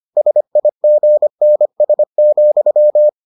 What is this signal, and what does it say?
Morse code, heard as a keyed tone, about 25 wpm, 600 Hz. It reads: SIGNS,